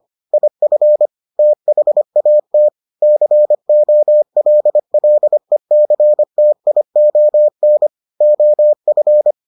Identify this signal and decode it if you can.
Morse code, heard as a keyed tone, about 25 wpm, 600 Hz. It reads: IF THAT COLLECTION OF